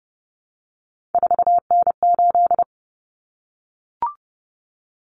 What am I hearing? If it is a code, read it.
Morse code, heard as a keyed tone, about 30 wpm, 700 Hz. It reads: 4N8
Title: Common Cut Numbers - 3-Digit 30wpm